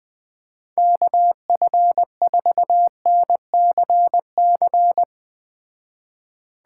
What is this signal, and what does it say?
Morse code, heard as a keyed tone, about 19 wpm, 700 Hz. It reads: KF4NCC
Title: Speed Racing - USA 2x3 Call Signs 17wpm